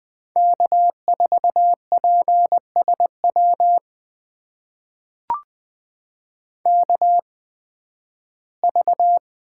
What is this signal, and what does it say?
Morse code, heard as a keyed tone, about 20 wpm, spaced as for 13 wpm, 700 Hz. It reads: K4PSW KV
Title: USA Call Signs - 5 Characters - 8x Character Spacing - 20wpm